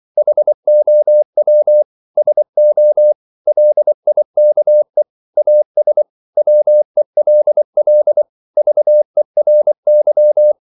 Morse code, heard as a keyed tone, about 24 wpm, 600 Hz. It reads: HOW SO LIKE AS WELL VERY
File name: CW / Practice / 500MostCommonEnglishWords24